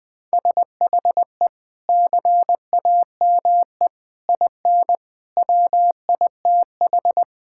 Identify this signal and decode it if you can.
Morse code, heard as a keyed tone, about 20 wpm, 700 Hz. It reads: SHE CAME IN WITH
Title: Sentences with 4-Word Phrases - Spoken Beginning - 20wpm